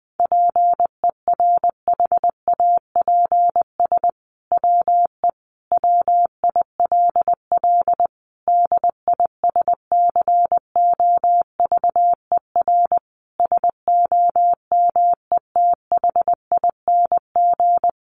Morse code, heard as a keyed tone, about 20 wpm, 700 Hz. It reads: PERHAPS WE WILL DISCOVER SOMETHING
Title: Sentences from Top 1100 Words 20wpm